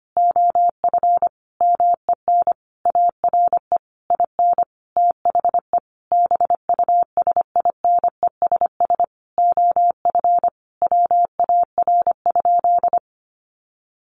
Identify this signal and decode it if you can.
Morse code, heard as a keyed tone, about 25 wpm, 700 Hz. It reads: OF MEN ARE IN THE BUSINESS OF WAR?
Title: Sentences from Top 1300 Words 25wpm